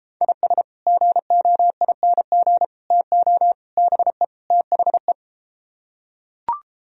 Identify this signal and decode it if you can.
Morse code, heard as a keyed tone, about 32 wpm, 700 Hz. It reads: IS GOING TO BE THE